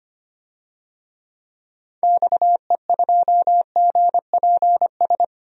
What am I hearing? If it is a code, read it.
Morse code, heard as a keyed tone, about 25 wpm, 700 Hz. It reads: XE2GPS
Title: DXCC-100 Call Signs - 6 Characters - 25wpm